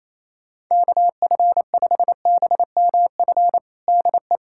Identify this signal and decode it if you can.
Morse code, heard as a keyed tone, about 28 wpm, 700 Hz. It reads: KF5BMF DE
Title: USA Callsign DE Callsign - 6 Characters - 28wpm